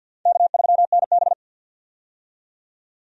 Morse code, heard as a keyed tone, about 51 wpm, 700 Hz. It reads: K3NB
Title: USA Call Signs - 4 Characters - 50wpm